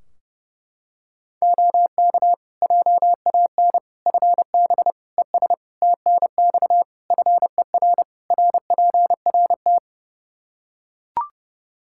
Morse code, heard as a keyed tone, about 30 wpm, 700 Hz. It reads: OK JAN FB ES TNX FER RPRT
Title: K1USN SST QSO Practice 30wpm